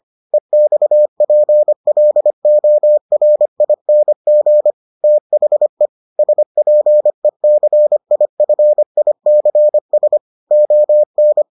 Morse code, heard as a keyed tone, about 25 wpm, 600 Hz. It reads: EXPLORING THE SPECIFICS ON